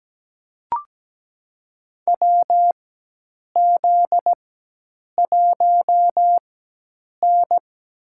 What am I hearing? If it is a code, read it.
Morse code, heard as a keyed tone, about 17 wpm, spaced as for 8 wpm, 700 Hz. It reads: WZ1N
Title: USA Call Signs - 4 Characters - 4x Character Spacing - 17wpm